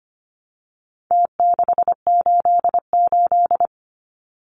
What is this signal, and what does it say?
Morse code, heard as a keyed tone, about 25 wpm, 700 Hz. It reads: T688
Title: Numbers - Contest Style - 4-Digit 25wpm